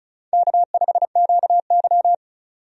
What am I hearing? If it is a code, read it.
Morse code, heard as a keyed tone, about 36 wpm, 700 Hz. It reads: K5QY
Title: USA Call Signs - 4 Characters - 35wpm